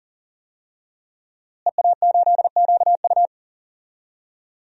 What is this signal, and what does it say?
Morse code, heard as a keyed tone, about 39 wpm, 700 Hz. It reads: EA8QU
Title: DXCC-100 Call Signs - 5 Characters - Repeated 2x - 40wpm